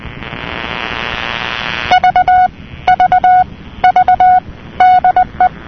Morse code, heard as a keyed tone, about 20 wpm, 730 Hz. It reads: V V V DE
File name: europe / nor / nor-237-ptt jeloey'lcj,...'-cw-(1985)-(rb)-(mc-909-2-10)